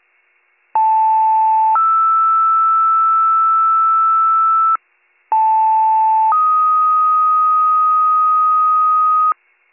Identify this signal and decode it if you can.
Two-tone page, heard as a one-second tone, then a three-second tone, about 880 then 1320 Hz, about 880 then 1250 Hz. It is heard from 0.7 to 9.3 s.